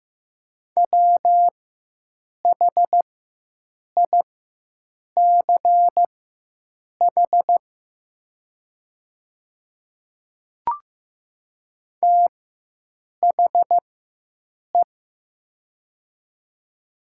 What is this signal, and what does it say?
Morse code, heard as a keyed tone, about 15 wpm, spaced as for 7 wpm, 700 Hz. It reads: WHICH THE
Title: 4x Character Spacing - Top 100 Words - 15wpm